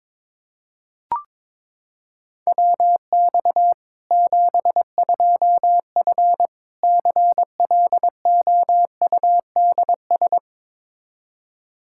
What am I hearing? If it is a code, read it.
Morse code, heard as a keyed tone, about 22 wpm, 700 Hz. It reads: WX 72F CLOUDS
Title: K1USN SST QSO Practice 22wpm